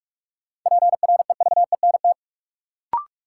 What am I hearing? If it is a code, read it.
Morse code, heard as a keyed tone, about 46 wpm, 700 Hz. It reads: PREVENT
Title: Sentences from Top 1100 Words - New Word Drill Alt - 45wpm